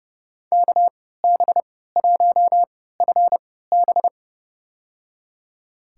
Morse code, heard as a keyed tone, about 32 wpm, spaced as for 17 wpm, 700 Hz. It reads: KB1FB
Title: USA Call Signs - 5 Characters - 3x Character Spacing - 30wpm